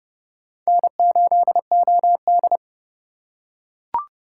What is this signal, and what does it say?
Morse code, heard as a keyed tone, about 27 wpm, 700 Hz. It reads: N8OD